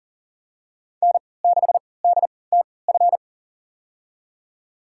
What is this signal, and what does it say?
Morse code, heard as a keyed tone, about 40 wpm, spaced as for 23 wpm, 700 Hz. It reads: N6DTF
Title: USA Call Signs - 5 Characters - 3x Character Spacing - 40wpm